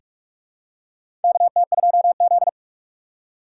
Morse code, heard as a keyed tone, about 46 wpm, 700 Hz. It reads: KT2Z
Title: USA Call Signs - 3x Character Spacing - 45wpm